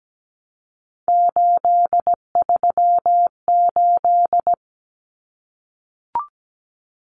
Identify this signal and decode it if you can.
Morse code, heard as a keyed tone, about 17 wpm, 700 Hz. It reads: 838
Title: Numbers - 3-Digit 17wpm